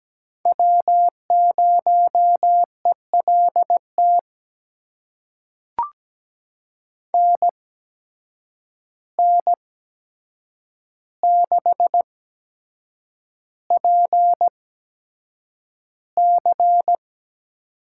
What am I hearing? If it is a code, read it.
Morse code, heard as a keyed tone, about 17 wpm, spaced as for 7 wpm, 700 Hz. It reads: W0ELT NN6PC
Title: USA Call Signs - 8x Character Spacing - 17wpm